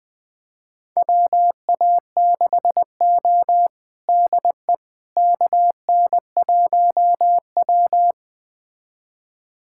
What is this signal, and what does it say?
Morse code, heard as a keyed tone, about 20 wpm, 700 Hz. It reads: WA6O DE KN1W
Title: USA Callsign DE Callsign - 4 Characters - 20wpm